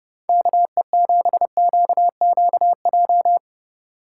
Morse code, heard as a keyed tone, about 38 wpm, 700 Hz. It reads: KE7QQJ